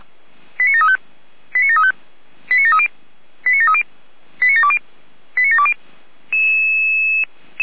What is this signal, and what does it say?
Five-tone selective call (ZVEI), heard as quick runs of five tones, tones of about 70 ms. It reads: 89735 89735 89730 89730 89720 89720
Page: SDR/ZVEI